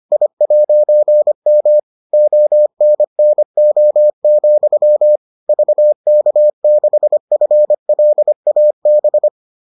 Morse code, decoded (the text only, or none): I'M ONNO, VK6FLAB